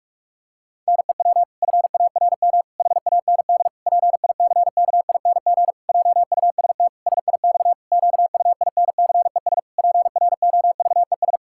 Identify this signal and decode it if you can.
Morse code, heard as a keyed tone, about 46 wpm, 700 Hz. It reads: NEW FARM HAND PICKING JUST SIX QUINCES PROVES
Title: Pangram Challenge v2 - 45wpm